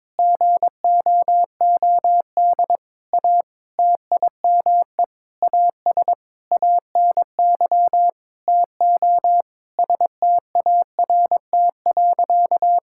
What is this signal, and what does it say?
Morse code, heard as a keyed tone, about 22 wpm, 700 Hz. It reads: GOOD A TIME AS ANY TO START.
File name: Sentences with 7-Word Phrases - 22wpm